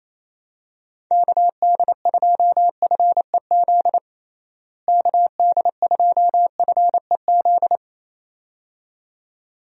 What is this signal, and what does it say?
Morse code, heard as a keyed tone, about 28 wpm, 700 Hz. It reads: KD2FEZ KD2FEZ
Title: USA Call Signs - 6 Characters - Repeated 2x - 28wpm